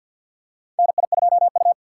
Morse code, heard as a keyed tone, about 53 wpm, 700 Hz. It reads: NI2U
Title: USA Call Signs - 4 Characters - 3x Character Spacing - 50wpm